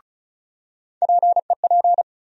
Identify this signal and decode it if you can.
Morse code, heard as a keyed tone, about 36 wpm, 700 Hz. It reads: PEP
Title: USA Call Sign Postfixes - 3 Letters 35wpm